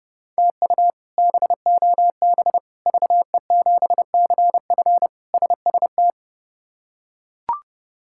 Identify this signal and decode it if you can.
Morse code, heard as a keyed tone, about 30 wpm, 700 Hz. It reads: TU BOB VE7CF SST
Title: K1USN SST QSO Practice - 30wpm